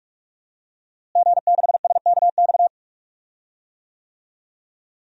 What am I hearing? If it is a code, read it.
Morse code, heard as a keyed tone, about 46 wpm, 700 Hz. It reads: G6SKX
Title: DXCC-100 Call Signs - 5 Characters - Repeated 2x - 45wpm